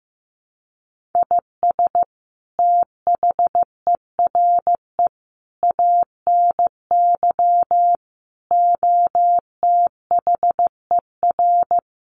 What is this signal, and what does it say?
Morse code, heard as a keyed tone, about 15 wpm, 700 Hz. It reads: IS THERE ANY OTHER
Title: Sentences from Top 1200 Words 15wpm